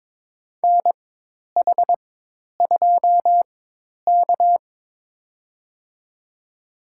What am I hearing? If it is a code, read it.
Morse code, heard as a keyed tone, about 22 wpm, spaced as for 10 wpm, 700 Hz. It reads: NH2K
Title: USA Call Signs - 4 Characters - 4x Character Spacing - 22wpm